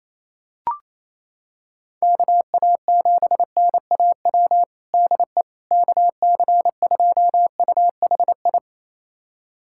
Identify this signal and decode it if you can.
Morse code, heard as a keyed tone, about 28 wpm, 700 Hz. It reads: KA7NAW DE KC2UHI
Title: USA Callsign DE Callsign 28wpm